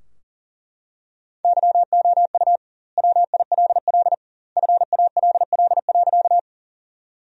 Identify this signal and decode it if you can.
Morse code, heard as a keyed tone, about 40 wpm, 700 Hz. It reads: YOU WILL FALL.
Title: 2x Word-Spacing - Sentences from Top 500 Words - 40wpm